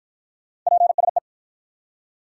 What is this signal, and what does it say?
Morse code, heard as a keyed tone, about 53 wpm, 700 Hz. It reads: PSE